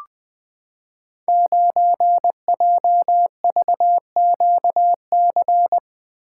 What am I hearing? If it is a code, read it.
Morse code, heard as a keyed tone, about 20 wpm, 700 Hz. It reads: 9JVQC
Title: Code Groups - 5 Letter:Numbers - 20wpm